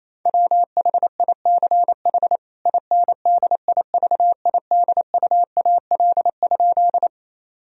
Morse code, heard as a keyed tone, about 28 wpm, 700 Hz. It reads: WHICH INDIVIDUAL?
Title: Sentences from Top 1000 Words - Review Part 4 - 28wpm